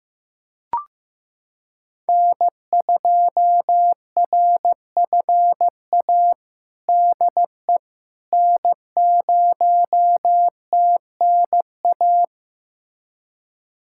Morse code, heard as a keyed tone, about 15 wpm, 700 Hz. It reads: N2RFA DE N0TNA
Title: USA Callsign DE Callsign - 5 Characters - 15wpm